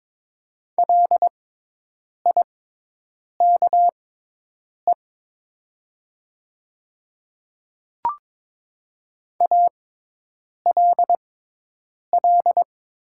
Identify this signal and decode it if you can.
Morse code, heard as a keyed tone, about 22 wpm, spaced as for 8 wpm, 700 Hz. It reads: LIKE ALL
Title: 6x Character Spacing - Top 100 Words - 22wpm